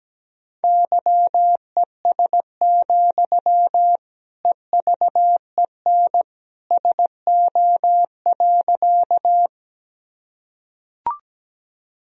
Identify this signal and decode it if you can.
Morse code, heard as a keyed tone, about 17 wpm, 700 Hz. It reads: YES, EVEN SO.
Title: Sentences from Top 1000 Words - Review Part 1 - 17wpm